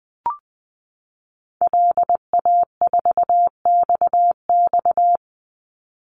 Morse code, heard as a keyed tone, about 20 wpm, 700 Hz. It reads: LA4XX
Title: Call Signs - 20wpm